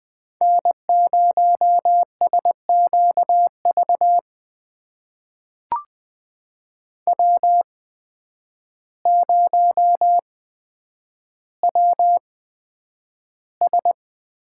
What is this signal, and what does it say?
Morse code, heard as a keyed tone, about 20 wpm, spaced as for 9 wpm, 700 Hz. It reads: N0SQV W0WS
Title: USA Call Signs - 5 Characters - 8x Character Spacing - 20wpm